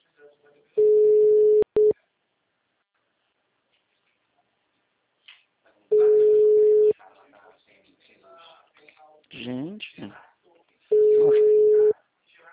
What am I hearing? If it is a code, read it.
Ringback tone (Europe, 425 Hz), with speech.
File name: out- -2033- - - .2394